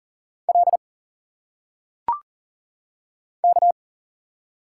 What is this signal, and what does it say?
Morse code, heard as a keyed tone, about 40 wpm, 700 Hz. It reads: L K
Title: Single Letters - Phonetic Pronunciation 40wpm